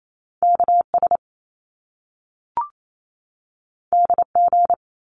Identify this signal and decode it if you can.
Morse code, heard as a keyed tone, about 28 wpm, 700 Hz. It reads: KS DG
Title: Top Two Letter Combinations in Words 28wpm